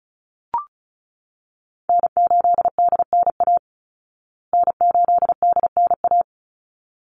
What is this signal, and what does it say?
Morse code, heard as a keyed tone, about 34 wpm, 700 Hz. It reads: N8DNA N8DNA
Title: Call Signs Repeated 3x - 35wpm